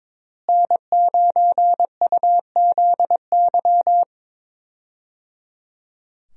Morse code, heard as a keyed tone, about 22 wpm, 700 Hz. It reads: N9UZY
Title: USA Call Signs - 5 Characters - Repeated 3x - 22wpm